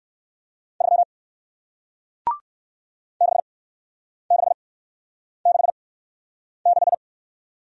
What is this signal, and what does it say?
Morse code, heard as a keyed tone, about 57 wpm, spaced as for 9 wpm, 700 Hz. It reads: 3 6666